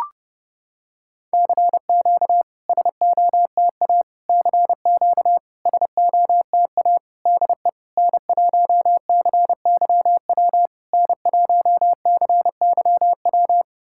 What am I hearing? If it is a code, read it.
Morse code, heard as a keyed tone, about 30 wpm, 690 Hz. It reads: CQ SOTA CQ SOTA DE N1CYW N1CYW
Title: SOTA QSO Practice 30wpm